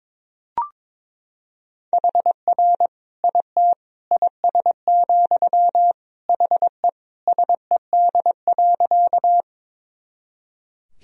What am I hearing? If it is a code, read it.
Morse code, heard as a keyed tone, about 22 wpm, 700 Hz. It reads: HR IT IS, HE SED.
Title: Sentences from Top 200 Words and Abbreviations 22wpm